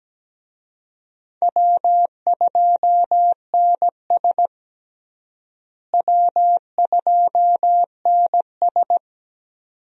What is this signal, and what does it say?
Morse code, heard as a keyed tone, about 17 wpm, 700 Hz. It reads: W2NS W2NS
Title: USA Call Signs - 4 Characters - Repeated 2x - 17wpm